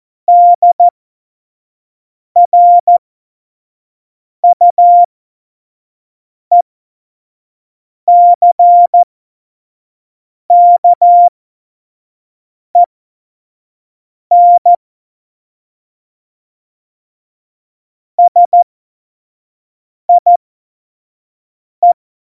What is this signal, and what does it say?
Morse code, heard as a keyed tone, about 14 wpm, spaced as for 5 wpm, 700 Hz. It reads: DRUECKEN SIE
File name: Morsekurs / Witze / Witz 29. Januar 2019, 25 F70